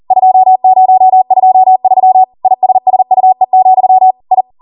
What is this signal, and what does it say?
Morse code, heard as a keyed tone, about 40 wpm, 750 Hz. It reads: 2023 ISSUE, I